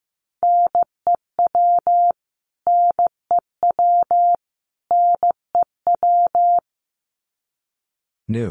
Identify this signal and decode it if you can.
Morse code, heard as a keyed tone, about 15 wpm, 700 Hz. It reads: NEW NEW NEW
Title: Top 500 Words Repeated 15wpm